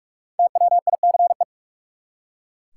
Morse code, heard as a keyed tone, about 44 wpm, 700 Hz. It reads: TWICE
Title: Five Letter Words - 45wpm